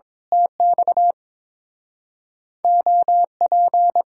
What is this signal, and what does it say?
Morse code, heard as a keyed tone, about 22 wpm, 700 Hz. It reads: TX OP
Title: Speed-Racing Common QSO Phrases - 22wpm